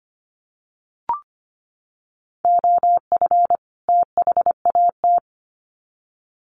Morse code, heard as a keyed tone, about 25 wpm, 700 Hz. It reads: OF THAT
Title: Sets of 2 Words 25wpm